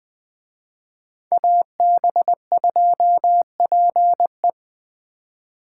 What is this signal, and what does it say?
Morse code, heard as a keyed tone, about 20 wpm, 700 Hz. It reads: AB2PE
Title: Speed Racing - USA Call Signs 17wpm